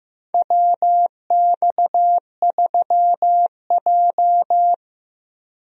Morse code, heard as a keyed tone, about 15 wpm, 700 Hz. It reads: WX3J
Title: USA Call Signs - 4 Characters - Repeated 2x - 15wpm